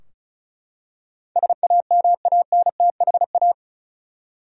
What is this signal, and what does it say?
Morse code, heard as a keyed tone, about 36 wpm, 700 Hz. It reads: SAMANTHA
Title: USA Common Names - 35wpm